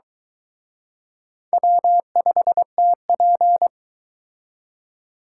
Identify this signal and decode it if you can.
Morse code, heard as a keyed tone, about 22 wpm, 700 Hz. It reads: W5TP